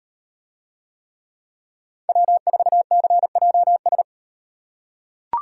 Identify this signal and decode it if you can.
Morse code, heard as a keyed tone, about 34 wpm, 700 Hz. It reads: W4CJS